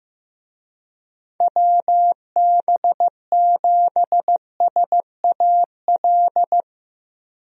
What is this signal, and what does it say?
Morse code, heard as a keyed tone, about 16 wpm, 700 Hz. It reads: WB7SAL
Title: USA Call Signs - 6 Characters - 15wpm